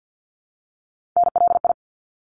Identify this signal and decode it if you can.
Morse code, heard as a keyed tone, about 50 wpm, 700 Hz. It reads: NLI
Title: ARRL Sections 50wpm